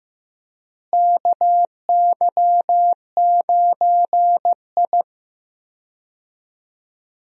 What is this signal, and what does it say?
Morse code, heard as a keyed tone, about 15 wpm, 700 Hz. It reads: KY9I
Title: USA Call Signs - 4 Characters - Repeated 3x - 15wpm